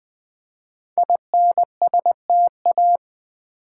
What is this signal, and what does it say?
Morse code, heard as a keyed tone, about 20 wpm, 700 Hz. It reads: INSTA
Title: Build A Word - 5 Letters At A Time - 12-Letter Words - 20wpm